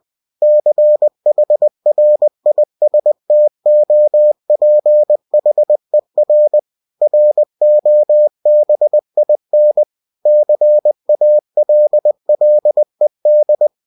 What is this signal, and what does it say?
Morse code, heard as a keyed tone, about 20 wpm, 600 Hz. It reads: CHRISTOPHER ROBIN CALLED
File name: cw-books / winnie-the-pooh / winnie-20wpm-page-0083